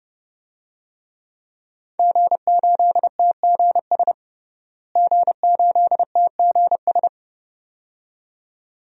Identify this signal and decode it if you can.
Morse code, heard as a keyed tone, about 30 wpm, 700 Hz. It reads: G8TGS G8TGS
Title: DXCC-100 Call Signs - 5 Characters - Repeated 2x - 30wpm